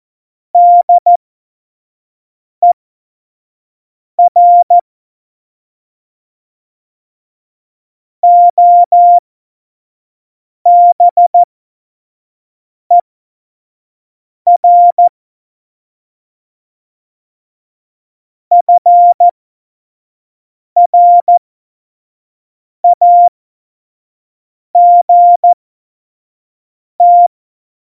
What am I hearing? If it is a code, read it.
Morse code, heard as a keyed tone, about 14 wpm, spaced as for 5 wpm, 700 Hz. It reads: DER OBER FRAGT
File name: Morsekurs / Witze / Witz 29. Oktober 2019, 25 F70